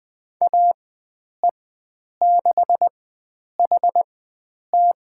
Morse code, heard as a keyed tone, about 20 wpm, spaced as for 9 wpm, 700 Hz. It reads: AE6HT
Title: USA Call Signs - 5 Characters - 4x Character Spacing - 20wpm